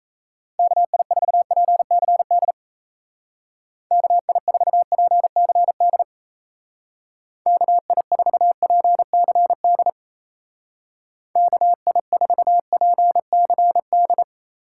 Morse code, heard as a keyed tone, about 36 wpm, 700 Hz. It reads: KI4PCD KI4PCD KI4PCD KI4PCD